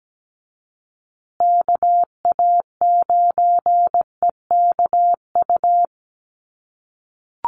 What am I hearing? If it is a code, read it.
Morse code, heard as a keyed tone, about 17 wpm, 700 Hz. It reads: KA9EKU
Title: USA Call Signs Repeated 2x 17wpm